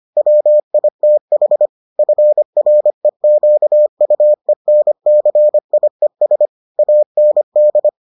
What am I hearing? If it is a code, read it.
Morse code, heard as a keyed tone, about 25 wpm, 600 Hz. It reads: WITH FREQUENCIES AND